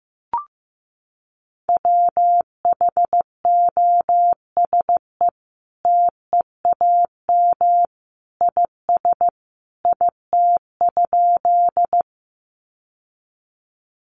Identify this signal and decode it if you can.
Morse code, heard as a keyed tone, about 15 wpm, 700 Hz. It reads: WHOSE TEAM IS IT?
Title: Sentences from Top 1500 Words 15wpm